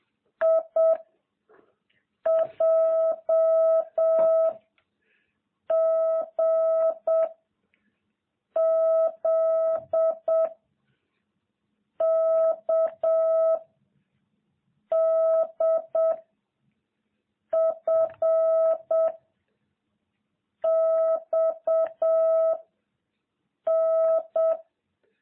Morse code, heard as a keyed tone, about 7 wpm, 660 Hz. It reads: I J G Z K D F X N